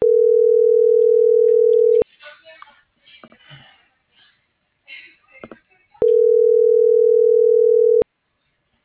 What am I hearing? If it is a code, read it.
Ringback tone (North America), 440 + 480 Hz.